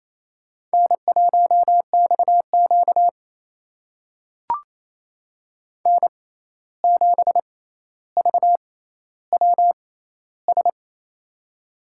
Morse code, heard as a keyed tone, about 28 wpm, spaced as for 13 wpm, 700 Hz. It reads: N1XQ N7VWS
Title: USA Call Signs - 6x Character Spacing - 28wpm